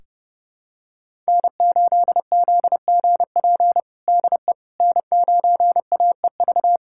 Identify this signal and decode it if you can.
Morse code, heard as a keyed tone, about 30 wpm, 700 Hz. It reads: N8ZGP DE N9AEV
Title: USA Callsign DE Callsign - 5 Characters - 30wpm